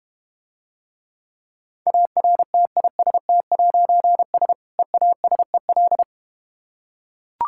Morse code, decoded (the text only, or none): ARTIST'S EASEL